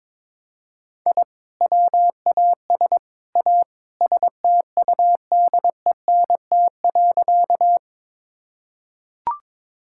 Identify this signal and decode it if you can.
Morse code, heard as a keyed tone, about 22 wpm, 700 Hz. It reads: I WAS A STUDENT.
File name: practice / 22wpm / Sentences with 3-Word Phrases - 22wpm